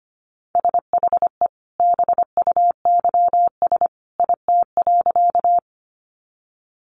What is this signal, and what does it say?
Morse code, heard as a keyed tone, about 25 wpm, 700 Hz. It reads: SHE BUYS IT.
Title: Additional Easy Phrases 25wpm